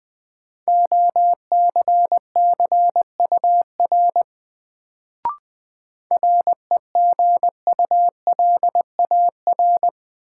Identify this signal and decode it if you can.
Morse code, heard as a keyed tone, about 20 wpm, 700 Hz. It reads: OCCUR REGULAR
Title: Sentences from Top 900 Words - New Word Drill - 20wpm